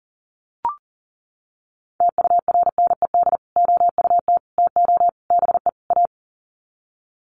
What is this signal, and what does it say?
Morse code, heard as a keyed tone, about 38 wpm, 700 Hz. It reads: TURNED OUT TO BE A
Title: Sets of 5 Words 40wpm